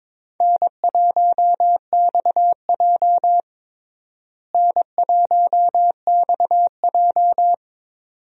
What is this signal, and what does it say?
Morse code, heard as a keyed tone, about 22 wpm, 700 Hz. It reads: N1XJ N1XJ
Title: USA Call Signs - 4 Characters - Repeated 2x - 22wpm